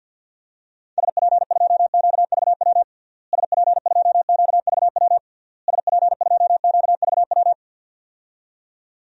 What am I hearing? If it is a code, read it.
Morse code, heard as a keyed tone, about 50 wpm, 700 Hz. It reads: SP2QVW SP2QVW SP2QVW
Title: DXCC-100 Call Signs - 6 Characters - Repeated 3x - 50wpm